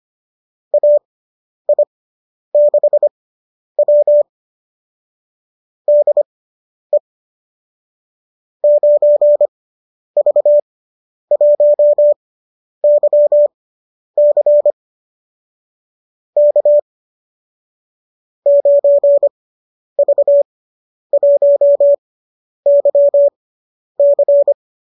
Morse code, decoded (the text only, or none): AI6W DE 9V1YC K 9V1YC